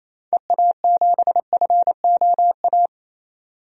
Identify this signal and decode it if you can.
Morse code, heard as a keyed tone, about 28 wpm, 700 Hz. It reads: EA7FOA
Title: DXCC-100 Call Signs - 6 Characters - 28wpm